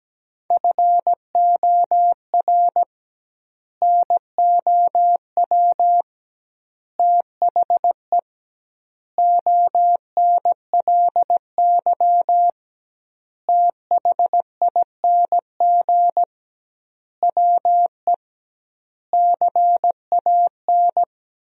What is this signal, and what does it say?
Morse code, heard as a keyed tone, about 17 wpm, 700 Hz. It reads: FOR NOW THE ONLY THING WE CAN
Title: 2x Word-Spacing - Sentences from Top 200 Words - 17wpm